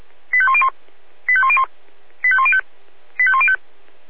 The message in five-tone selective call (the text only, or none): 85101 85101 85105 85105